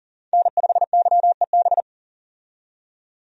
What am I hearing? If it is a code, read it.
Morse code, heard as a keyed tone, about 40 wpm, 700 Hz. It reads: N5YEB
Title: USA Call Signs - 5 Characters - 40wpm